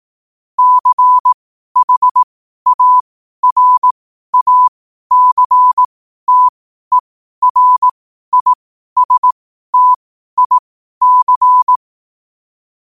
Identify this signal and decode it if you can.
Morse code, heard as a keyed tone, about 18 wpm, spaced as for 12 wpm, 1000 Hz. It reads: CHARACTERISTIC
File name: qotdfiles / 12wpm